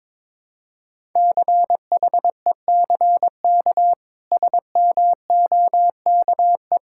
Morse code, heard as a keyed tone, about 22 wpm, 700 Hz. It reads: CHECK SMOKE
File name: practice / 22wpm / Easy Sentences with Hints - 22wpm